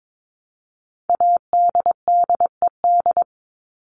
Morse code, heard as a keyed tone, about 22 wpm, 700 Hz. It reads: ADDED
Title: ED Suffix - Top 100 Words with Base Words 22wpm